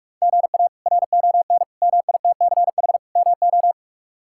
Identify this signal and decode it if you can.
Morse code, heard as a keyed tone, about 46 wpm, 700 Hz. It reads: GA RON MITCH MO
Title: K1USN SST QSO Practice - 45wpm